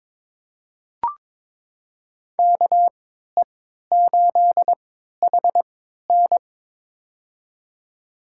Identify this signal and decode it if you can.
Morse code, heard as a keyed tone, about 22 wpm, spaced as for 12 wpm, 700 Hz. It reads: KE8HN
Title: USA Call Signs - 5 Characters - 3x Character Spacing - 22wpm